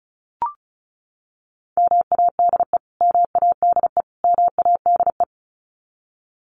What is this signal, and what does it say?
Morse code, heard as a keyed tone, about 34 wpm, 700 Hz. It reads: MADE MADE MADE
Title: Top 500 Words Repeated 35wpm